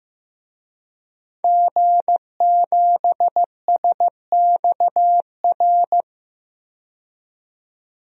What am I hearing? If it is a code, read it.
Morse code, heard as a keyed tone, about 16 wpm, 700 Hz. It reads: G7SXR
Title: DXCC-100 Call Signs - 5 Characters - 15wpm